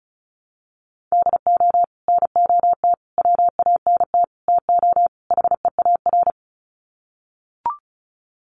Morse code, heard as a keyed tone, about 34 wpm, 700 Hz. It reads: DO NOT WANT TO HEAR
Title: Sets of 5 Words 35wpm